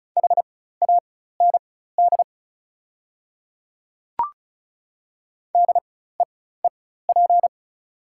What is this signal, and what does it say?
Morse code, heard as a keyed tone, about 36 wpm, spaced as for 16 wpm, 690 Hz. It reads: HAND DEEP
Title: 4x Character Spacing - Top 500 Words - 35wpm